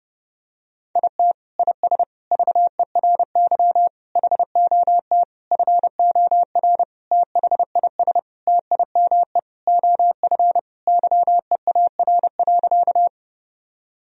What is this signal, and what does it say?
Morse code, heard as a keyed tone, about 32 wpm, 700 Hz. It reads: IT IS VERY HOT FOR THIS TIME OF YEAR.
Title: Sentences from Top 1000 Words - Review Part 3 - 30wpm